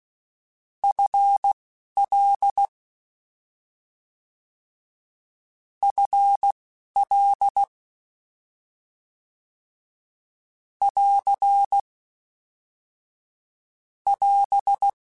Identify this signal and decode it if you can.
Morse code, heard as a keyed tone, about 16 wpm, spaced as for 11 wpm, 800 Hz. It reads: FL FL <AR> <AS>